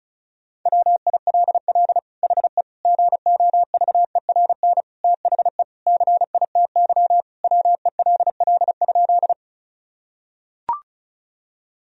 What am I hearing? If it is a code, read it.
Morse code, heard as a keyed tone, about 36 wpm, 700 Hz. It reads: WILL HE GOVERN THE CITY WELL?
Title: Sentences from Top 1000 Words - Review Part 2 - 35wpm